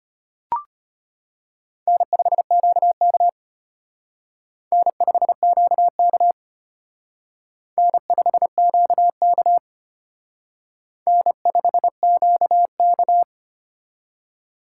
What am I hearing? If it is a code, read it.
Morse code, heard as a keyed tone, about 32 wpm, 700 Hz. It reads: N5QK N5QK N5QK N5QK